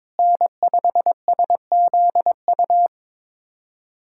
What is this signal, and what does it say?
Morse code, heard as a keyed tone, about 22 wpm, 700 Hz. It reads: N5SZU
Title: USA Callsign DE Callsign - 5 Characters - 22wpm